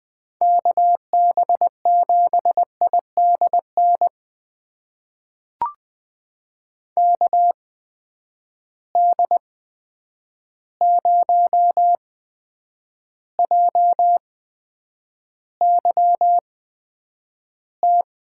Morse code, heard as a keyed tone, about 20 wpm, spaced as for 9 wpm, 700 Hz. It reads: KB7IDN KD0JYT
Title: USA Call Signs - 6 Characters - 8x Character Spacing - 20wpm